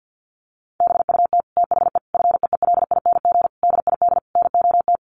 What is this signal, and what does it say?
Morse code, heard as a keyed tone, about 50 wpm, 700 Hz. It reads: BUT THE FEELING DID NOT
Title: Sentences from Top 1200 Words 50wpm